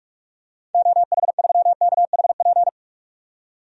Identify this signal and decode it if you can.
Morse code, heard as a keyed tone, about 41 wpm, 700 Hz. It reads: OH3KHP
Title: DXCC-100 Call Signs - 6 Characters - 45wpm